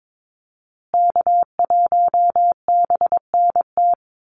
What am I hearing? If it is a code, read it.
Morse code, heard as a keyed tone, about 22 wpm, 700 Hz. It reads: K1BNT
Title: USA Call Signs Repeated 3x 22wpm